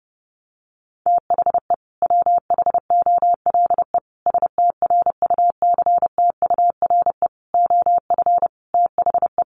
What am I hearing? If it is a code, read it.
Morse code, heard as a keyed tone, about 30 wpm, 700 Hz. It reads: THE WHOLE STRUCTURE OF THE
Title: Sentences from Top 1500 Words 30wpm